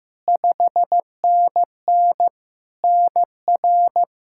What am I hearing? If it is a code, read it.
Morse code, heard as a keyed tone, about 15 wpm, 700 Hz. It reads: 5NN NR
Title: Practice QSO - Exchange 1 - Beginning - 15wpm